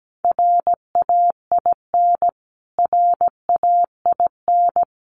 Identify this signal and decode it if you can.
Morse code, heard as a keyed tone, about 17 wpm, 700 Hz. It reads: RAIN RAIN
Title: Top 300 Words Repeated 17wpm